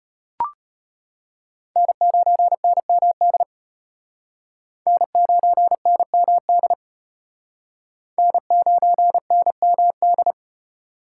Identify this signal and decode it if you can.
Morse code, heard as a keyed tone, about 31 wpm, 700 Hz. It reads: N9NMD N9NMD N9NMD
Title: Speed Racing - USA 1x3 Call Signs 25wpm